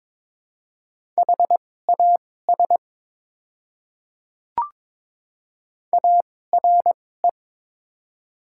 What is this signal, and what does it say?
Morse code, heard as a keyed tone, about 22 wpm, spaced as for 16 wpm, 700 Hz. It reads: HAS ARE
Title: 2x Character Spacing - Top 100 Words - 22wpm